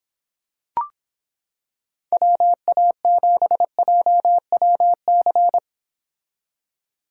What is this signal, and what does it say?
Morse code, heard as a keyed tone, about 26 wpm, 700 Hz. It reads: WA7JWC